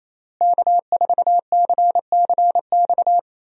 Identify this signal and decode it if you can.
Morse code, heard as a keyed tone, about 28 wpm, 700 Hz. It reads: K4CCX
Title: USA Call Signs - 5 Characters - Repeated 3x - 28wpm